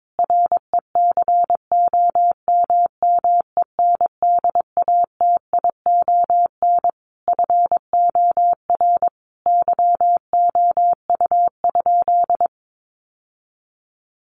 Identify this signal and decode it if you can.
Morse code, heard as a keyed tone, about 22 wpm, 700 Hz. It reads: RECOMMENDATION FOR YOU?
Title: Sentences from Top 1400 Words 22wpm